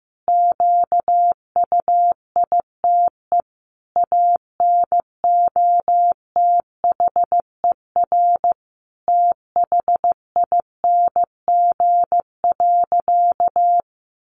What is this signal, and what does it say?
Morse code, heard as a keyed tone, about 15 wpm, 700 Hz. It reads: QUITE ANOTHER THING.
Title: Sentences from Top 700 Words 15wpm